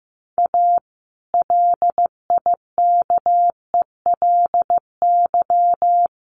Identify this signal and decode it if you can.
Morse code, heard as a keyed tone, about 15 wpm, 700 Hz. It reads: A LIKELY
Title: Sentences from Top 600 Words 15wpm